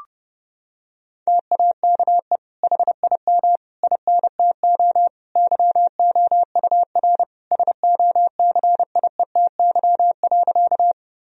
Morse code, heard as a keyed tone, about 30 wpm, 700 Hz. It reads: TAKE HIM INTO YOUR SOCIETY.
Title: Sentences from Top 1000 Words - Review Part 4 - 30wpm